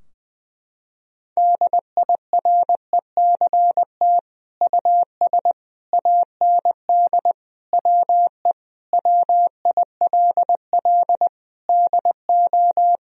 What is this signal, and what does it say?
Morse code, heard as a keyed tone, about 20 wpm, 700 Hz. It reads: DIRECT US AND WE WILL DO
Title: Sentences from Top 1000 Words - Review Part 2 - 20wpm